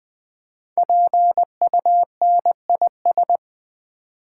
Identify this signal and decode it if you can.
Morse code, heard as a keyed tone, about 20 wpm, 700 Hz. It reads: PUNIS
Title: Build A Word - 5 Letters At A Time - 10-Letter Words - 20wpm